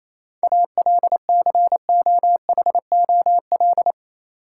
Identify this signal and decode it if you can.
Morse code, heard as a keyed tone, about 28 wpm, 700 Hz. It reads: ALCOHOL
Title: Sentences from Top 1400 Words - New Word Drill - 28wpm